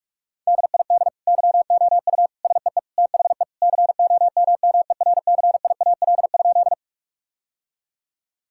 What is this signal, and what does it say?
Morse code, heard as a keyed tone, about 46 wpm, 690 Hz. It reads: DID YOU SEE THE COMMERCIAL?
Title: Sentences from Top 1000 Words - Review Part 5 - 45wpm